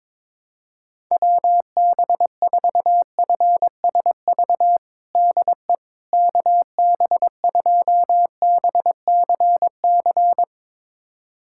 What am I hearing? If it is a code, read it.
Morse code, heard as a keyed tone, about 22 wpm, 700 Hz. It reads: WB4FSV DE KB2BCC
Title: USA Callsign DE Callsign - 6 Characters - 22wpm